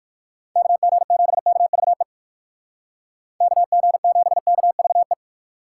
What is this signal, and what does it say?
Morse code, heard as a keyed tone, about 59 wpm, 700 Hz. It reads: KG7KVE KG7KVE